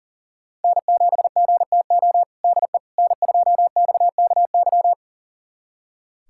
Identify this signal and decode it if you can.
Morse code, heard as a keyed tone, about 40 wpm, 700 Hz. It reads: N7GTO DE N2XKY
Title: USA Callsign DE Callsign - 5 Characters - 40wpm